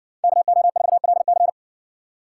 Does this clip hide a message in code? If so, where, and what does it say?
Yes: Morse code from 0.2 to 1.5 s, heard as a keyed tone, about 60 wpm, 700 Hz. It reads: KO4LC